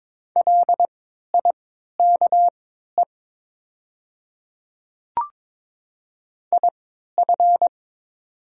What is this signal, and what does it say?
Morse code, heard as a keyed tone, about 22 wpm, spaced as for 12 wpm, 690 Hz. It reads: LIKE IF
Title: 3x Character Spacing - Top 100 Words - 22wpm